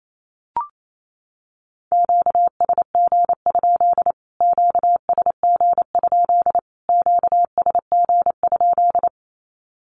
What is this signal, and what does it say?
Morse code, heard as a keyed tone, about 28 wpm, 700 Hz. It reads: QSG? QSG? QSG?
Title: Q-Codes Repeated 3x 28wpm